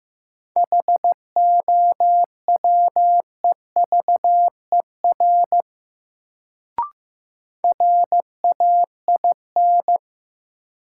Morse code, heard as a keyed tone, about 15 wpm, 700 Hz. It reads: HOWEVER RAIN
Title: Sentences from Top 600 Words - New Word Drill - 15wpm